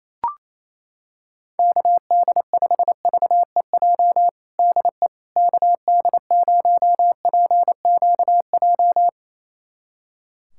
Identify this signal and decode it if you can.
Morse code, heard as a keyed tone, about 28 wpm, 700 Hz. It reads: KD5VEJ DE KD0PQJ
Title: USA Callsign DE Callsign - 6 Characters - 28wpm